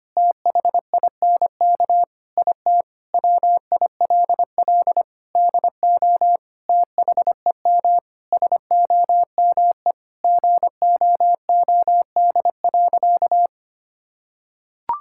Morse code, heard as a keyed tone, about 25 wpm, 700 Hz. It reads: THINK IT WILL DO THEM SOME GOOD.
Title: Sentences from Top 1000 Words - Review Part 6 - 25wpm